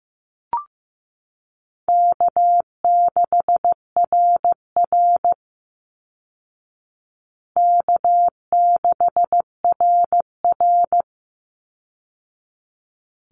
Morse code, heard as a keyed tone, about 15 wpm, 700 Hz. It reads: K6RR K6RR
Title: Call Signs Repeated 3x - 15wpm